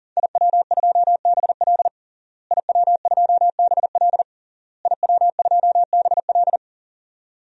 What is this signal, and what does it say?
Morse code, heard as a keyed tone, about 37 wpm, 700 Hz. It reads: IW2BL IW2BL IW2BL
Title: DXCC-100 Call Signs - 5 Characters - Repeated 3x - 40wpm